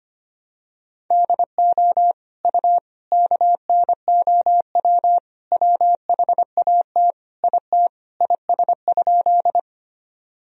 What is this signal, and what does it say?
Morse code, heard as a keyed tone, about 25 wpm, 700 Hz. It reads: DO U KNOW WHAT IT IS?
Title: Sentences from Top 100 Words and Abbreviations 25wpm